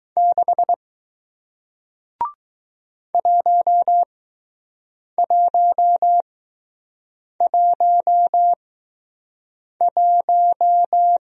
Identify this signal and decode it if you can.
Morse code, heard as a keyed tone, about 20 wpm, spaced as for 6 wpm, 700 Hz. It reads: 6 1111